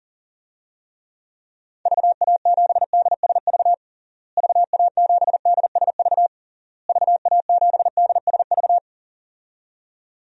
Morse code, heard as a keyed tone, about 37 wpm, 700 Hz. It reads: VA7DSV VA7DSV VA7DSV
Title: DXCC-100 Call Signs - 6 Characters - Repeated 3x - 40wpm